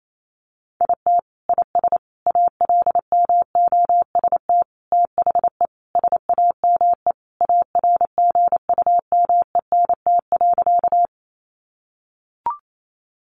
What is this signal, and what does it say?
Morse code, heard as a keyed tone, about 28 wpm, 700 Hz. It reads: IT IS ALMOST THE SAME ARGUMENT.
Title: Sentences from Top 1400 Words 28wpm